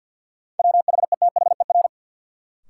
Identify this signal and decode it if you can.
Morse code, heard as a keyed tone, about 47 wpm, 700 Hz. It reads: WHETHER
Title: Sets of 3 Words - New Word Drill - 50wpm